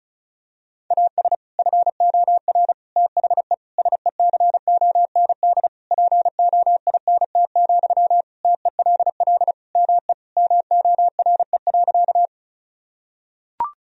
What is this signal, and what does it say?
Morse code, heard as a keyed tone, about 34 wpm, 700 Hz. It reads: AS FOR THE SECOND POINT, TELL ME MORE.
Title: Sentences from Top 1000 Words - Review Part 1 - 35wpm